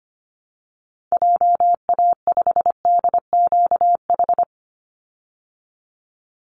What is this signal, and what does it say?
Morse code, heard as a keyed tone, about 25 wpm, 700 Hz. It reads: JA5DQH
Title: Call Signs Repeated 3x - 25wpm